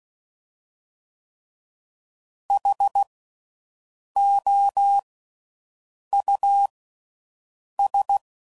Morse code, heard as a keyed tone, about 16 wpm, spaced as for 6 wpm, 800 Hz. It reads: HOUS
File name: Suffixes / General / Houses